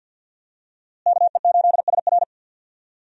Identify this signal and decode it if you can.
Morse code, heard as a keyed tone, about 52 wpm, 700 Hz. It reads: KE8SR
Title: USA Call Signs - 5 Characters - Repeated 3x - 50wpm